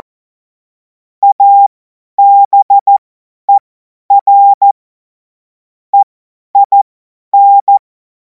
Morse code, heard as a keyed tone, about 14 wpm, spaced as for 10 wpm, 800 Hz. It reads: ABER EIN
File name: Morsekurs / Witze / Witz 10. März 2026 50 F70